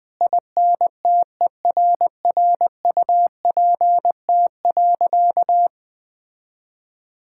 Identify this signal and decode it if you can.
Morse code, heard as a keyed tone, about 20 wpm, 700 Hz. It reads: INTERRUPT.
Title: Sentences with 5-Word Phrases - 20wpm